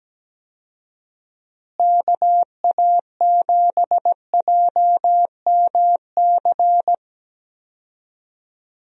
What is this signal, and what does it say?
Morse code, heard as a keyed tone, about 17 wpm, 700 Hz. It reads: KA7JMC
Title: USA Call Signs - 6 Characters - Repeated 2x - 17wpm